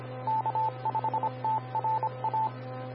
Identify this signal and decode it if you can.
Morse code, heard as a keyed tone, about 25 wpm, 880 Hz. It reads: K5TRA
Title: Austin, TX Amateur Repeaters